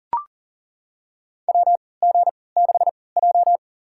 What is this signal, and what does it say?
Morse code, heard as a keyed tone, about 40 wpm, spaced as for 23 wpm, 700 Hz. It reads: WG6J
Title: USA Call Signs - 4 Characters - 3x Character Spacing - 40wpm